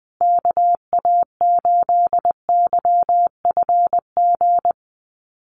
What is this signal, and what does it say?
Morse code, heard as a keyed tone, about 20 wpm, 700 Hz. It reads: KA8YFG
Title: USA Call Signs Repeated 2x 20wpm